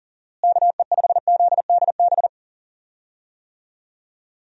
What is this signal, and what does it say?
Morse code, heard as a keyed tone, about 37 wpm, 700 Hz. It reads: KE5ZDB
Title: USA Call Signs - 6 Characters - Repeated 2x - 40wpm